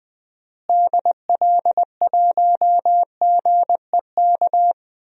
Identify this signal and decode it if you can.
Morse code, heard as a keyed tone, about 20 wpm, 700 Hz. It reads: DL1GEK
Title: DXCC-100 Call Signs - 6 Characters - Repeated 2x - 20wpm